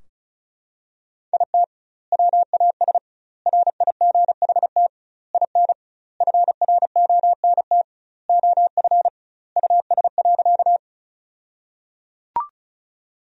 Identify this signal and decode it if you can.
Morse code, heard as a keyed tone, about 36 wpm, 700 Hz. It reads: IT WAS RIGHT IN FRONT OF US.
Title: 2x Word-Spacing - Sentences from Top 500 Words - 35wpm